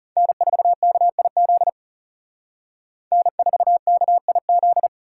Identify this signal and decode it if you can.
Morse code, heard as a keyed tone, about 37 wpm, 700 Hz. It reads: N4KIZ N4KIZ
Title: Speed Racing - USA 1x3 Call Signs 30wpm